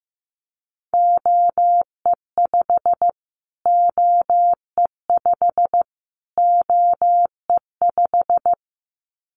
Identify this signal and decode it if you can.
Morse code, heard as a keyed tone, about 15 wpm, 700 Hz. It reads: OE5 OE5 OE5
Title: Callsign Prefixes for Top 25 DXCC - Repeated 3x - 15wpm